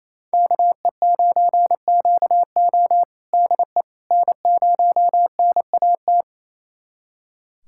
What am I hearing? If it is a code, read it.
Morse code, heard as a keyed tone, about 28 wpm, 700 Hz. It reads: KE9QO DE N0NAT
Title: USA Callsign DE Callsign - 5 Characters - 28wpm